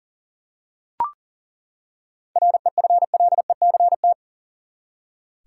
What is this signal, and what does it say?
Morse code, heard as a keyed tone, about 40 wpm, 700 Hz. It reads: REFLECT
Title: Seven Letter Words - 40wpm